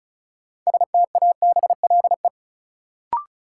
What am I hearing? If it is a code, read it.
Morse code, heard as a keyed tone, about 36 wpm, 700 Hz. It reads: STABLE
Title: Sentences from Top 1500 Words - New Word Drill Alt - 35wpm